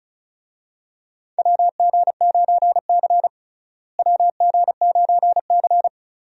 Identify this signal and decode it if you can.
Morse code, heard as a keyed tone, about 36 wpm, 700 Hz. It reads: WG9C WG9C
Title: USA Call Signs - 4 Characters - Repeated 3x - 35wpm